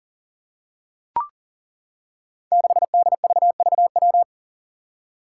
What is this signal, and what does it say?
Morse code, heard as a keyed tone, about 40 wpm, 700 Hz. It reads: 6DVVW
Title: Five Challenging Letter-Number Combinations 40wpm